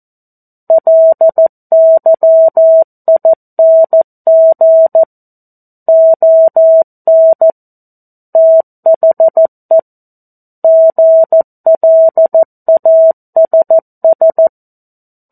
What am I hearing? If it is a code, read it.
Morse code, heard as a keyed tone, about 14 wpm, 650 Hz. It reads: LYING ON THE GLASS
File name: 02-Chapter-02 14 WPM